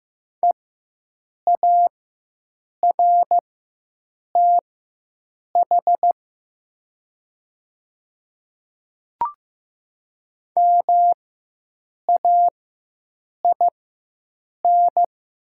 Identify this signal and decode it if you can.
Morse code, heard as a keyed tone, about 15 wpm, spaced as for 7 wpm, 700 Hz. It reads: EARTH MAIN
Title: 4x Character Spacing - Top 300 Words - 15wpm